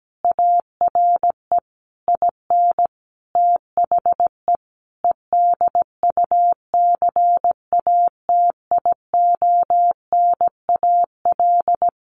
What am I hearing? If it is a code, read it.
Morse code, heard as a keyed tone, about 17 wpm, 700 Hz. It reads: ARE IN THE EDUCATIONAL
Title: Sentences from Top 1200 Words 17wpm